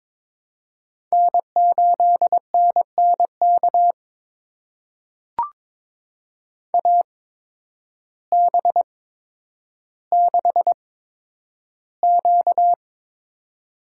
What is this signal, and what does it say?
Morse code, heard as a keyed tone, about 22 wpm, spaced as for 10 wpm, 700 Hz. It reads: N8NNK AB6Q
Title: USA Call Signs - 8x Character Spacing - 22wpm